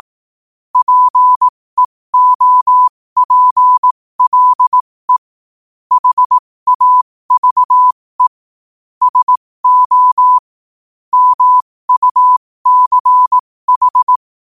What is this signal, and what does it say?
Morse code, heard as a keyed tone, about 18 wpm, spaced as for 15 wpm, 1000 Hz. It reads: PEOPLE HAVE SO MUCH
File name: qotdfiles / 15wpm